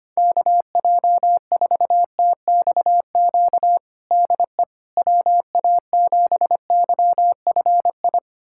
Morse code, heard as a keyed tone, about 25 wpm, 700 Hz. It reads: KJ4TXQ DE WA7YFI
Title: USA Callsign DE Callsign - 6 Characters - 25wpm